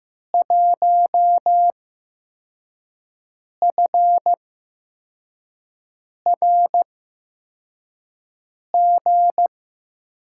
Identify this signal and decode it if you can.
Morse code, heard as a keyed tone, about 16 wpm, spaced as for 4 wpm, 700 Hz. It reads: 1FRG
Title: USA Call Signs - 6 Characters - 8x Character Spacing - 15wpm